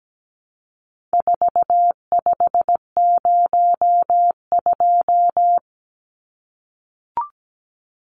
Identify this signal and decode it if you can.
Morse code, heard as a keyed tone, about 17 wpm, 700 Hz. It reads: 4502
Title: Numbers - 4-Digit 17wpm